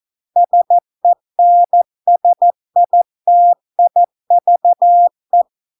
Morse code, heard as a keyed tone, about 14 wpm, 700 Hz. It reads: SENSITIVE